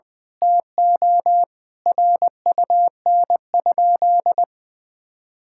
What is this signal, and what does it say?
Morse code, heard as a keyed tone, about 20 wpm, 700 Hz. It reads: TO RUN?
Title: Sentences from Top 1000 Words - Review Part 6 - 20wpm